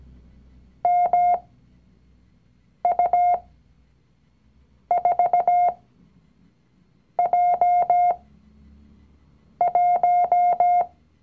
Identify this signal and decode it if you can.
Morse code, heard as a keyed tone, about 17 wpm, spaced as for 5 wpm, 710 Hz. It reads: MU4J1
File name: CW-Online-Kurs / Anfänger / Lektion 019